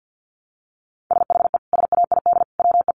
Morse code, heard as a keyed tone, about 49 wpm, 700 Hz. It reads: SHE SAID WE
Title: Sentences from Top 1300 Words 50wpm